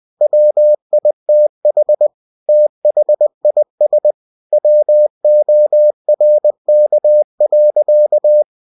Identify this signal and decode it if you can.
Morse code, heard as a keyed tone, about 20 wpm, 600 Hz. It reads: WITH THIS WORK.
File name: cw-books / winnie-the-pooh / winnie-20wpm-page-0114